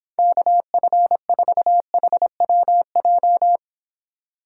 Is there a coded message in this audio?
Yes: Morse code from 0.2 to 3.6 s, heard as a keyed tone, about 29 wpm, 700 Hz. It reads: KF4HWJ